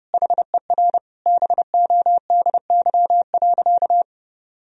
Morse code, heard as a keyed tone, about 30 wpm, 700 Hz. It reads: HER BODY.
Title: Sentences from Top 1000 Words - Review Part 2 - 30wpm